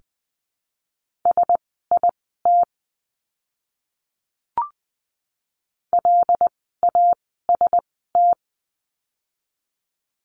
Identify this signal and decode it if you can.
Morse code, heard as a keyed tone, about 20 wpm, spaced as for 14 wpm, 700 Hz. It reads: SIT LAST
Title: 108 Lesson 18f - Letter m - Words 20wpm